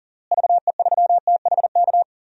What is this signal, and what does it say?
Morse code, heard as a keyed tone, about 40 wpm, 700 Hz. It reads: VE3THK
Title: DXCC-100 Call Signs - 6 Characters - Repeated 3x - 40wpm